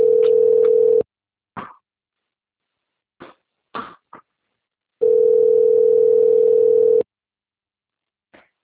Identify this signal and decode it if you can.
Ringback tone (North America), 440 + 480 Hz.
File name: rg-1600-unknown- - - .38219